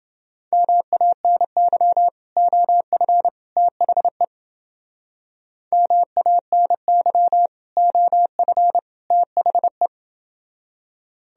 Morse code, heard as a keyed tone, about 30 wpm, 700 Hz. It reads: MANY OF THE MANY OF THE